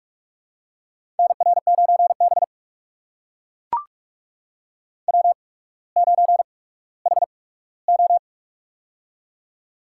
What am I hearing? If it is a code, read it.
Morse code, heard as a keyed tone, about 44 wpm, spaced as for 19 wpm, 700 Hz. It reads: NR9B W9HO
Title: USA Call Signs - 4 Characters - 8x Character Spacing - 45wpm